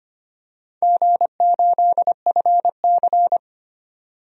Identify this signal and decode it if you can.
Morse code, heard as a keyed tone, about 25 wpm, 700 Hz. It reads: G8FC
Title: DXCC-100 Call Signs - 4 Characters - 25wpm